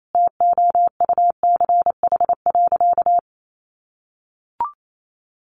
Morse code, heard as a keyed tone, about 28 wpm, 700 Hz. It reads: TOUCH.
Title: Sentences from Top 1500 Words 28wpm